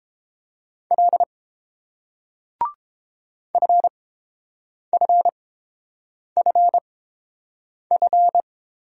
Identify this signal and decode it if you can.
Morse code, heard as a keyed tone, about 30 wpm, spaced as for 7 wpm, 700 Hz. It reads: L FFFF